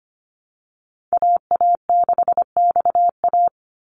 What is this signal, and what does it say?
Morse code, heard as a keyed tone, about 25 wpm, 700 Hz. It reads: AA6XA
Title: Call Signs - 25wpm